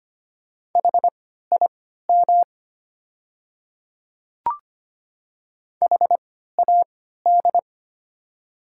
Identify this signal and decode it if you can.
Morse code, heard as a keyed tone, about 25 wpm, spaced as for 14 wpm, 700 Hz. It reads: HIM HAD
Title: 3x Character Spacing - Top 100 Words - 25wpm